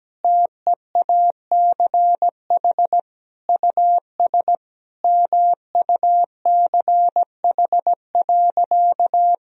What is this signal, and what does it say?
Morse code, heard as a keyed tone, about 17 wpm, 700 Hz. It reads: TEACH US MUCH.
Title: Sentences from Top 1000 Words - Review Part 2 - 17wpm